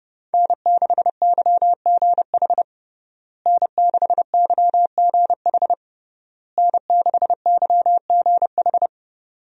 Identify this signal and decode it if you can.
Morse code, heard as a keyed tone, about 29 wpm, 700 Hz. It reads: N6YGH N6YGH N6YGH
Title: USA Call Signs - 5 Characters - Repeated 3x - 30wpm